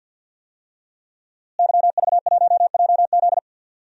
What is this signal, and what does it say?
Morse code, heard as a keyed tone, about 48 wpm, 700 Hz. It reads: YV1JZ
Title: DXCC-100 Call Signs - 5 Characters - Repeated 3x - 50wpm